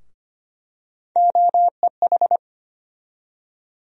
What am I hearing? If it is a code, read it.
Morse code, heard as a keyed tone, about 25 wpm, 700 Hz. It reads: OEH
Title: USA Call Sign Postfixes - 3 Letters 25wpm